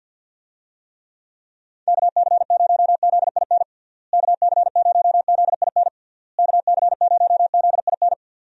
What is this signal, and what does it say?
Morse code, heard as a keyed tone, about 48 wpm, 700 Hz. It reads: KC0ZIN KC0ZIN KC0ZIN
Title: USA Call Signs - 6 Characters - Repeated 3x - 50wpm